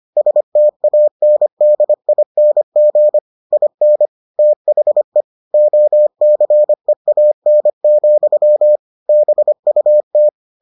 Morse code, heard as a keyed tone, about 25 wpm, 600 Hz. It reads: STANDING IN THE OCEAN, BUT